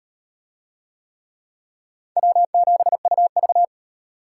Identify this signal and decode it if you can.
Morse code, heard as a keyed tone, about 36 wpm, 700 Hz. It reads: W7UV